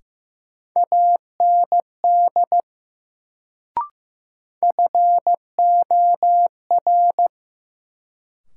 Morse code, heard as a keyed tone, about 15 wpm, 700 Hz. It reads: AND FOR
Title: Sentences from Top 100 Words - New Word Drill - 15wpm